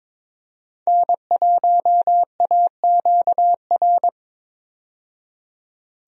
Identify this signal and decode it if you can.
Morse code, heard as a keyed tone, about 22 wpm, 700 Hz. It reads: N1AQR
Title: USA Call Signs - 5 Characters - 22wpm